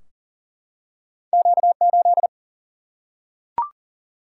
Morse code, heard as a keyed tone, about 40 wpm, 700 Hz. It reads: Q8
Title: Single-Letter Single-Number 40wpm